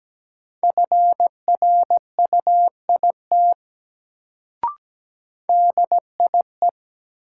Morse code, heard as a keyed tone, about 17 wpm, 700 Hz. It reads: FRUIT DIE
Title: Sentences from Top 1300 Words - New Word Drill - 17wpm